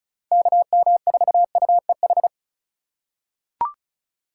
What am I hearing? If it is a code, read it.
Morse code, heard as a keyed tone, about 36 wpm, 700 Hz. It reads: KM4UEH
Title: USA Call Signs - 6 Characters - 2x Character Spacing - 35wpm